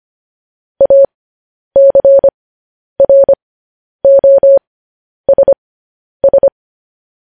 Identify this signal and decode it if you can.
Morse code, heard as a keyed tone, about 25 wpm, spaced as for 10 wpm, 550 Hz. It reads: ACROSS
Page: news headlines in morse code at 10 wpm